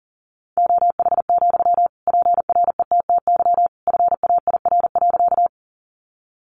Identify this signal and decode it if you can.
Morse code, heard as a keyed tone, about 40 wpm, 700 Hz. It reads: OH, PRETTY FAIR.
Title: Sentences from Top 1300 Words 40wpm